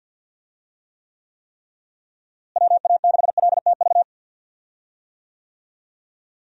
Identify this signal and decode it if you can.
Morse code, heard as a keyed tone, about 50 wpm, 700 Hz. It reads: WA6LTV
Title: USA Call Signs - 6 Characters - Repeated 3x - 50wpm